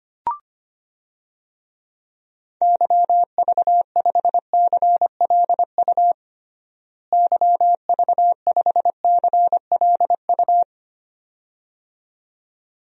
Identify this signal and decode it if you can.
Morse code, heard as a keyed tone, about 25 wpm, 700 Hz. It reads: YV5CLU YV5CLU
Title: DXCC-100 Call Signs - 6 Characters - Repeated 2x - 25wpm